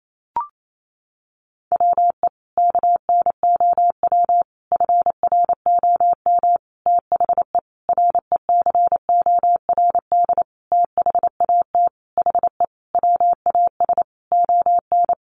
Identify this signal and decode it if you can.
Morse code, heard as a keyed tone, about 28 wpm, 700 Hz. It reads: WE KNOW FROM THE RECORD THAT HE WAS ON
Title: Sentences from Top 500 Words 28wpm